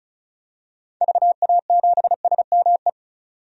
Morse code, heard as a keyed tone, about 36 wpm, 700 Hz. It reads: VA7SME
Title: DXCC-100 Call Signs - 6 Characters - 35wpm